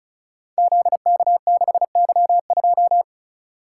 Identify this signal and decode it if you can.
Morse code, heard as a keyed tone, about 35 wpm, 700 Hz. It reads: ZK6Y2
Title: Code Groups - 5 Letter:Numbers - 35wpm